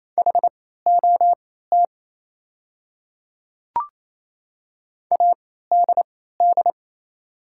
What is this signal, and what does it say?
Morse code, heard as a keyed tone, about 28 wpm, spaced as for 16 wpm, 700 Hz. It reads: HOT ADD
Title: 3x Character Spacing - Top 300 Words - 28wpm